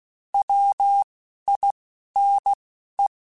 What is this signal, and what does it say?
Morse code, heard as a keyed tone, about 16 wpm, 800 Hz. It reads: W I N E
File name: Alphabetical / Alphabetical mp3s / Ws @10